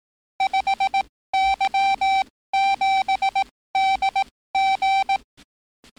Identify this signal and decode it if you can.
Morse code, heard as a keyed tone, about 18 wpm, spaced as for 15 wpm, 780 Hz. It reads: 5Y7DG
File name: training / NAVPERS 11049to11060 / NAVPERS 11056 RA RB